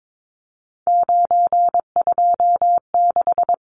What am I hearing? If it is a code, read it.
Morse code, heard as a keyed tone, about 22 wpm, 700 Hz. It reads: 926
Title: Numbers - 3-Digit 22wpm